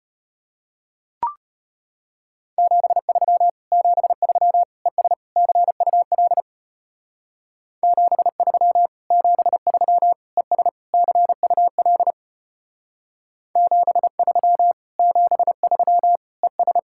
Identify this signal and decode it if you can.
Morse code, heard as a keyed tone, about 30 wpm, 700 Hz. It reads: 73 73 ES CUL 73 73 ES CUL 73 73 ES
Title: Speed-Racing Common QSO Phrases - 25wpm